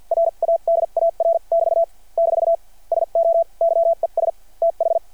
Morse code, heard as a keyed tone, about 51 wpm, spaced as for 41 wpm, 660 Hz. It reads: RADAR <BT> - SO YES TH
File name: archived / CRSnet